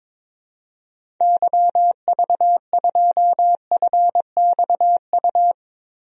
Morse code, heard as a keyed tone, about 22 wpm, 700 Hz. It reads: YV2FXU
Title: DXCC-100 Call Signs - 6 Characters - Repeated 2x - 22wpm